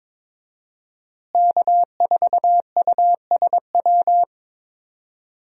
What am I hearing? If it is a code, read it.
Morse code, heard as a keyed tone, about 22 wpm, 700 Hz. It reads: K4USW
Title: USA Call Signs - 5 Characters - Repeated 3x - 22wpm